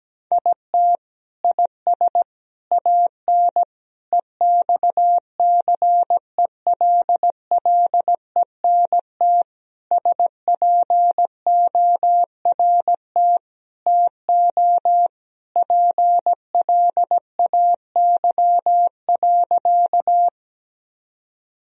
Morse code, heard as a keyed tone, about 17 wpm, 700 Hz. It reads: IT IS AN EXCELLENT SPORT TO PLAY.
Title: Sentences from Top 1000 Words - Review Part 4 - 17wpm